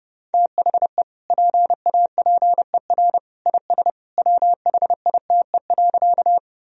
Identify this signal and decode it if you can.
Morse code, heard as a keyed tone, about 32 wpm, 700 Hz. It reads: THE PAPER IS WHITE.
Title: Sentences from Top 1000 Words - Review Part 1 - 30wpm